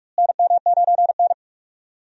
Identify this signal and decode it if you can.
Morse code, heard as a keyed tone, about 46 wpm, 700 Hz. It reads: NM9N
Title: USA Call Signs - 4 Characters - Repeated 2x - 45wpm